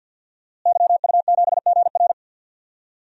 Morse code, heard as a keyed tone, about 51 wpm, 700 Hz. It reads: YU7GR
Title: DXCC-100 Call Signs - 5 Characters - Repeated 2x - 50wpm